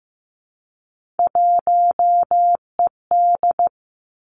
Morse code, heard as a keyed tone, about 15 wpm, 700 Hz. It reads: 1ED
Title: Three Challenging Letter-Number Combinations 15wpm